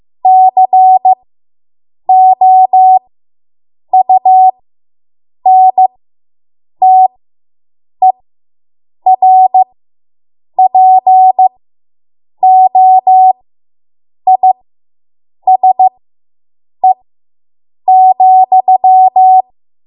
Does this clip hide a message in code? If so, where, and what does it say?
Yes: Morse code from 0.3 to 19.4 s, heard as a keyed tone, about 15 wpm, spaced as for 7 wpm, 750 Hz. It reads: COUNTERPOISE,